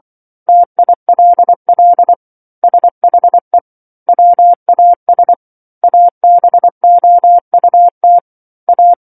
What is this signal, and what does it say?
Morse code, heard as a keyed tone, about 24 wpm, 700 Hz. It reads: TILL SHE WAS ABOUT A
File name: 07-Chapter-07 24 WPM